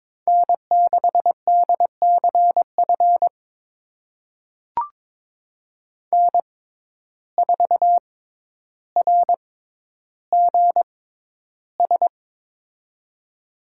Morse code, heard as a keyed tone, about 22 wpm, spaced as for 11 wpm, 700 Hz. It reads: N6DCF N4RGS
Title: USA Call Signs - 5 Characters - 6x Character Spacing - 22wpm